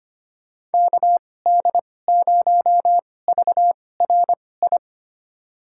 Morse code, heard as a keyed tone, about 25 wpm, spaced as for 18 wpm, 700 Hz. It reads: KD0VRI
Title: USA Call Signs - 6 Characters - 2x Character Spacing - 25wpm